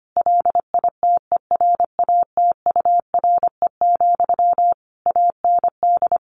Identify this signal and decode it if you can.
Morse code, heard as a keyed tone, about 25 wpm, 700 Hz. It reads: LITERATURE, AND
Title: Sentences from Top 600 Words 25wpm